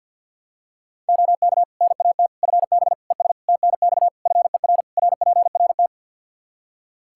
Morse code, heard as a keyed tone, about 47 wpm, 700 Hz. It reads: OK NAT FB ES TNX FER RPRT
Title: Practice QSO - Exchange 2 - Info - 50wpm